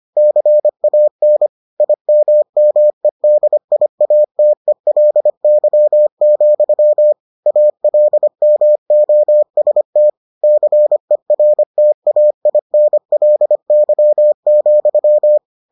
Morse code, heard as a keyed tone, about 25 wpm, 600 Hz. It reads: CAN IMMEDIATELY, ALMOST CERTAINLY,